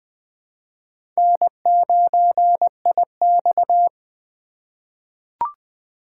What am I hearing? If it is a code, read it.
Morse code, heard as a keyed tone, about 21 wpm, 700 Hz. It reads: N9IX
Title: USA Call Signs - 4 Characters - 3x Character Spacing - 20wpm